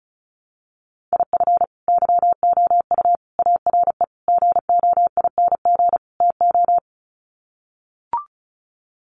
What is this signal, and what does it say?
Morse code, heard as a keyed tone, about 34 wpm, 700 Hz. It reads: IF YOU ARE GOING TO
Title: Sets of 5 Words Encore2 35wpm